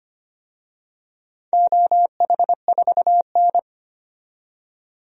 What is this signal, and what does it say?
Morse code, heard as a keyed tone, about 25 wpm, 700 Hz. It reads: OH4N
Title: DXCC-100 Call Signs - 4 Characters - 25wpm